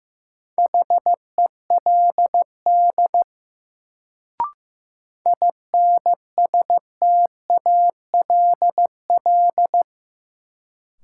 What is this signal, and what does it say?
Morse code, heard as a keyed tone, about 15 wpm, 700 Hz. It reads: HELD INSTALL
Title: Sentences from Top 1600 Words - New Word Drill Alt - 15wpm